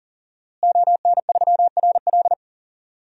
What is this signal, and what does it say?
Morse code, heard as a keyed tone, about 40 wpm, 700 Hz. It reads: ON3RL
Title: DXCC-100 Call Signs - 5 Characters - Repeated 3x - 40wpm